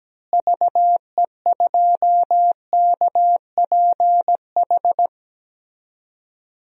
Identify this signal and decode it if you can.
Morse code, heard as a keyed tone, about 17 wpm, 700 Hz. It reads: VE2KPH
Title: DXCC-100 Call Signs - 6 Characters - Repeated 2x - 17wpm